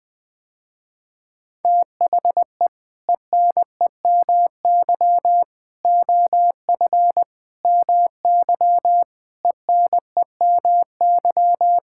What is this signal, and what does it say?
Morse code, heard as a keyed tone, about 20 wpm, 700 Hz. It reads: THE ENEMY OF MY ENEMY
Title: Sentences with 8-Word Phrases - Spoken Beginning - 20wpm